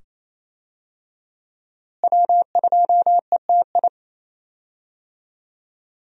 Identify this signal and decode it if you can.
Morse code, heard as a keyed tone, about 28 wpm, 700 Hz. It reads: W2ETI
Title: USA Call Signs - 5 Characters - Repeated 2x - 28wpm